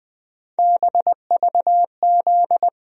Morse code, heard as a keyed tone, about 20 wpm, 700 Hz. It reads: BVZ
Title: USA Call Sign Postfixes - 3 Letters 20wpm